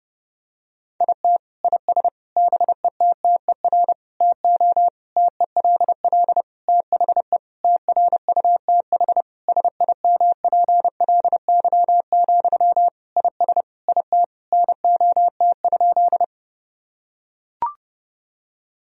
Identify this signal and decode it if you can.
Morse code, heard as a keyed tone, about 30 wpm, 700 Hz. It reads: IT IS BETTER TO TELL THE TRUTH SIMPLY, IS IT NOT?
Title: Sentences from Top 1000 Words - Review Part 4 - 30wpm